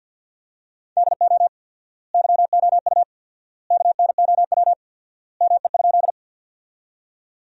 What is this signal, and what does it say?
Morse code, heard as a keyed tone, about 50 wpm, 700 Hz. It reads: DO YOU KNOW ME?
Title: 4x Word-Spacing - Sentences from Top 200 Words - 50wpm